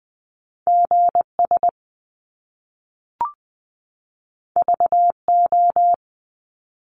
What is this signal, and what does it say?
Morse code, heard as a keyed tone, about 20 wpm, 700 Hz. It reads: G S V O
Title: Top Two Letter Combinations in Words 20wpm